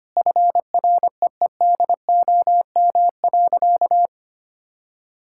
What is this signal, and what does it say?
Morse code, heard as a keyed tone, about 25 wpm, 700 Hz. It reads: FREEDOM.
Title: Sentences from Top 1000 Words - Review Part 5 - 25wpm